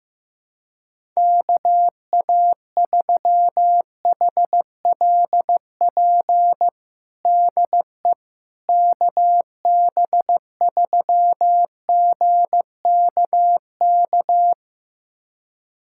Morse code, heard as a keyed tone, about 15 wpm, 700 Hz. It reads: KA3HLP DE KB3GKK
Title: USA Callsign DE Callsign 15wpm